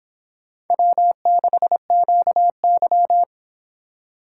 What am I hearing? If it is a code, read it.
Morse code, heard as a keyed tone, about 26 wpm, 700 Hz. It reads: W6QY